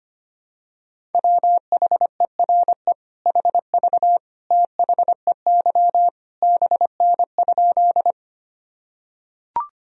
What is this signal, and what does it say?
Morse code, heard as a keyed tone, about 25 wpm, 700 Hz. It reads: WHERE HV THEY BN?
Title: Sentences from Top 200 Words and Abbreviations 25wpm